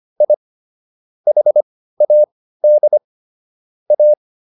Morse code, heard as a keyed tone, about 25 wpm, spaced as for 15 wpm, 600 Hz. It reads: I HAD A